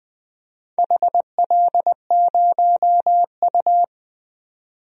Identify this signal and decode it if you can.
Morse code, heard as a keyed tone, about 21 wpm, 700 Hz. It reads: HL0U
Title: DXCC-100 Call Signs - 4 Characters - 20wpm